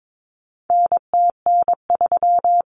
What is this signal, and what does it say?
Morse code, heard as a keyed tone, about 22 wpm, 700 Hz. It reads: NTN3
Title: Common Cut Numbers - 4-Digit 22wpm